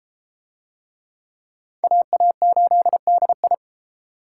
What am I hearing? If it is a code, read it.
Morse code, heard as a keyed tone, about 29 wpm, 700 Hz. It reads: AA8DI